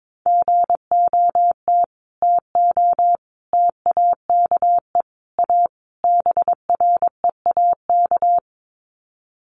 Morse code, heard as a keyed tone, about 22 wpm, 700 Hz. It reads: GOT TO TAKE A BREAK
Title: Sets of 5 Words Encore 22wpm